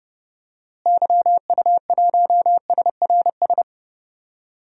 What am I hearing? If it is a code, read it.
Morse code, heard as a keyed tone, about 30 wpm, 700 Hz. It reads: YU1SRS
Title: DXCC-100 Call Signs - 6 Characters - Repeated 2x - 30wpm